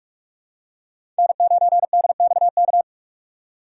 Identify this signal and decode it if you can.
Morse code, heard as a keyed tone, about 44 wpm, 700 Hz. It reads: N9DXK
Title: USA Call Signs - 5 Characters - 6x Character Spacing - 45wpm